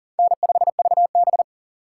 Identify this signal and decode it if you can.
Morse code, heard as a keyed tone, about 40 wpm, 700 Hz. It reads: N5VB
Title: USA Call Signs - 4 Characters - 6x Character Spacing - 40wpm